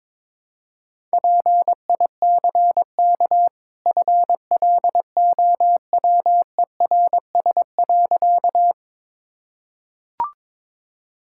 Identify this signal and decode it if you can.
Morse code, heard as a keyed tone, about 22 wpm, 700 Hz. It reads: PICK FLOWERS.
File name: practice / 22wpm / Easy Sentences with Hints - 22wpm